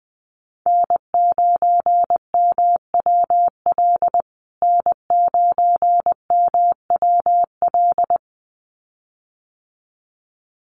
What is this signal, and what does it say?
Morse code, heard as a keyed tone, about 20 wpm, 700 Hz. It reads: N9MWL N9MWL
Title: USA Call Signs Repeated 2x 20wpm